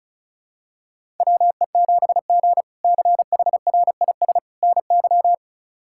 Morse code, heard as a keyed tone, about 34 wpm, 700 Hz. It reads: WE7G CHRIS NY
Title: K1USN SST QSO Practice - 35wpm